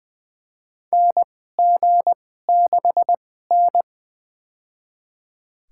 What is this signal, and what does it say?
Morse code, heard as a keyed tone, about 20 wpm, spaced as for 15 wpm, 700 Hz. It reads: NG6N
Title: USA Call Signs - 4 Characters - 2x Character Spacing - 20wpm